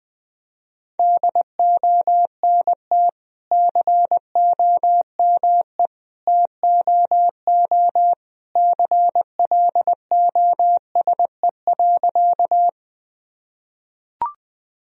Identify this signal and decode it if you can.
Morse code, heard as a keyed tone, about 20 wpm, 700 Hz. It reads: DONT COME TOO CLOSE.
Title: Sentences from Top 1000 Words - Review Part 1 - 20wpm